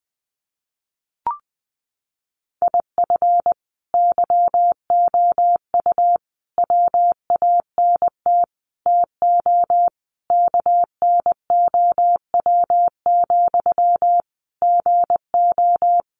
Morse code, heard as a keyed tone, about 20 wpm, 700 Hz. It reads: IF YOU WANT TO KNOW, GO
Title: Sentences from Top 200 Words 20wpm